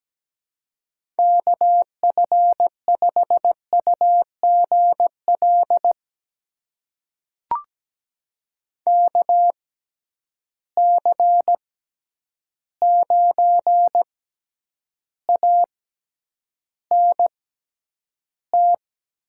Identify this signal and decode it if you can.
Morse code, heard as a keyed tone, about 17 wpm, spaced as for 9 wpm, 700 Hz. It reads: KF5UGL KC9ANT
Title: USA Call Signs - 6 Characters - 6x Character Spacing - 17wpm